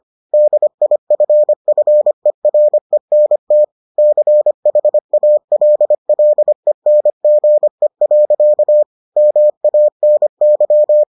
Morse code, heard as a keyed tone, about 25 wpm, 600 Hz. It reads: DIFFERENT CHALLENGE. MANY